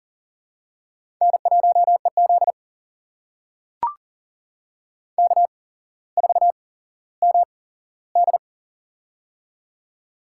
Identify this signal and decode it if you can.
Morse code, heard as a keyed tone, about 40 wpm, spaced as for 17 wpm, 700 Hz. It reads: N1EZ K4MD
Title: USA Call Signs - 4 Characters - 8x Character Spacing - 40wpm